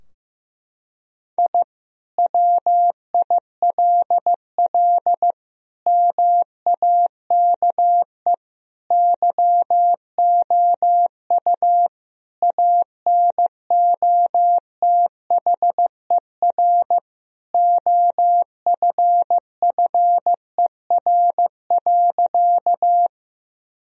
Morse code, heard as a keyed tone, about 15 wpm, 700 Hz. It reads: I WILL MAKE YOU ANOTHER OFFER.
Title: Sentences from Top 1000 Words - Review Part 4 - 15wpm